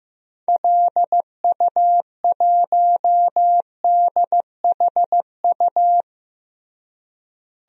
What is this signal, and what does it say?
Morse code, heard as a keyed tone, about 15 wpm, 700 Hz. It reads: LU1DHU
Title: DXCC-100 Call Signs - 6 Characters - Repeated 2x - 15wpm